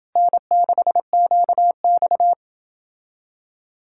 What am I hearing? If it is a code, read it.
Morse code, heard as a keyed tone, about 29 wpm, 700 Hz. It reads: N6QX